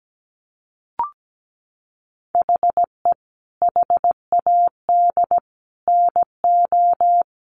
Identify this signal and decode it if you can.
Morse code, heard as a keyed tone, about 17 wpm, 700 Hz. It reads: HE HAD NO
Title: Sentences from Top 1200 Words 17wpm